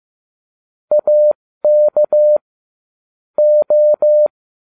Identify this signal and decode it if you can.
Morse code, heard as a keyed tone, about 15 wpm, spaced as for 13 wpm, 600 Hz. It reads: AK O